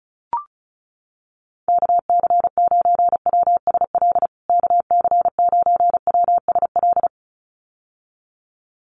Morse code, heard as a keyed tone, about 34 wpm, 700 Hz. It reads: KC9WSL KC9WSL
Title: USA Call Signs Repeated 2x 35wpm